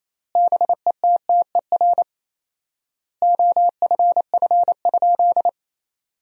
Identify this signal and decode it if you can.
Morse code, heard as a keyed tone, about 28 wpm, 700 Hz. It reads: BETTER OFF?
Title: 4x Word-Spacing - Sentences from Top 500 Words - 28wpm